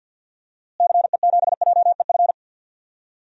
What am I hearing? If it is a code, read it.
Morse code, heard as a keyed tone, about 52 wpm, 700 Hz. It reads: CE7JEF
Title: DXCC-100 Call Signs - 6 Characters - Repeated 2x - 50wpm